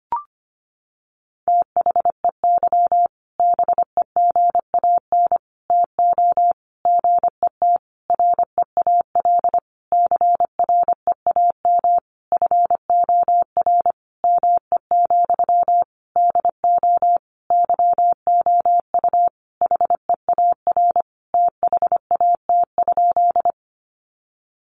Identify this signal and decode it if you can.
Morse code, heard as a keyed tone, about 25 wpm, 700 Hz. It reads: THEY BEGAN TO GET REAL CREAM FOR ME, DO YOU HEAR THAT?
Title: Sentences from Top 1500 Words 25wpm